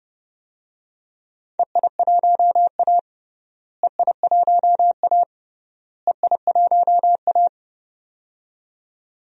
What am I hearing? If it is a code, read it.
Morse code, heard as a keyed tone, about 32 wpm, 700 Hz. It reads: EI1A EI1A EI1A
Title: DXCC-100 Call Signs - 4 Characters - Repeated 3x - 30wpm